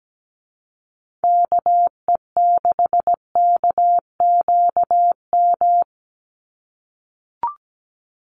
Morse code, heard as a keyed tone, about 17 wpm, 700 Hz. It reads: KE6KQM
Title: USA Call Signs 17wpm